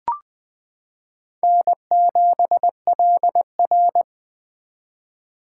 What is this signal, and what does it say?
Morse code, heard as a keyed tone, about 20 wpm, 700 Hz. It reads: N7LR
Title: K1USN SST QSO Practice - 20wpm